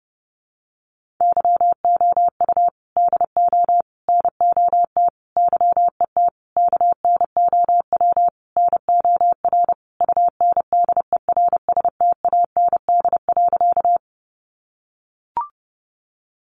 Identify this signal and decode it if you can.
Morse code, heard as a keyed tone, about 30 wpm, 700 Hz. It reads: YOU DO NOT YET KNOW NOR UNDERSTAND.
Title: Sentences from Top 1500 Words 30wpm